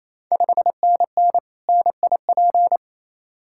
Morse code, heard as a keyed tone, about 28 wpm, 700 Hz. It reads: 5NN NIP
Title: Ontario QSO Party Exchange 28wpm